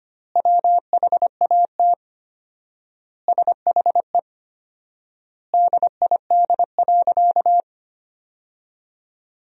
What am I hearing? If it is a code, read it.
Morse code, heard as a keyed tone, about 25 wpm, 700 Hz. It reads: WHAT SHE DID.
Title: 4x Word-Spacing - Sentences from Top 100 Words - 25wpm